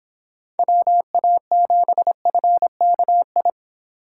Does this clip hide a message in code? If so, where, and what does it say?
Yes: Morse code from 0.6 to 3.5 s, heard as a keyed tone, about 26 wpm, 700 Hz. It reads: WA7FKI